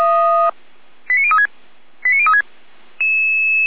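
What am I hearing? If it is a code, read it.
Five-tone selective call (ZVEI), heard as quick runs of five tones, tones of about 70 ms. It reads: 89036 89036
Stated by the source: ZVEI five-tone selective call received on an SDR.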